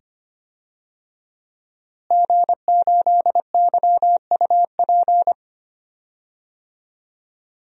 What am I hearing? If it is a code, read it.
Morse code, heard as a keyed tone, about 25 wpm, 700 Hz. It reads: G8YUP
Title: DXCC-100 Call Signs - 5 Characters - Repeated 2x - 25wpm